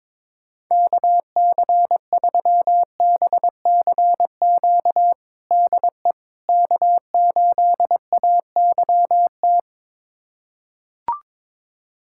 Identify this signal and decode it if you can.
Morse code, heard as a keyed tone, about 21 wpm, 700 Hz. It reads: KC3BCQ DE K8AYT
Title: USA Callsign DE Callsign 22wpm